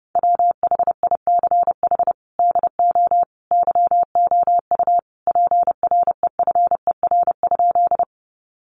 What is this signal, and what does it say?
Morse code, heard as a keyed tone, about 30 wpm, 700 Hz. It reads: WHICH DO YOU PREFER?
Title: Sentences from Top 1100 Words 30wpm